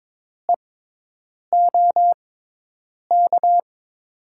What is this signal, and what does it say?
Morse code, heard as a keyed tone, about 22 wpm, spaced as for 8 wpm, 700 Hz. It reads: EOK
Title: USA Call Signs - 6x Character Spacing - 22wpm